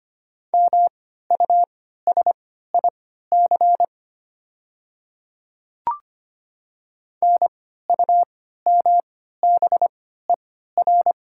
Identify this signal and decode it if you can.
Morse code, heard as a keyed tone, about 25 wpm, spaced as for 14 wpm, 700 Hz. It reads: MUSIC NUMBER
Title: 3x Character Spacing - Top 300 Words - 25wpm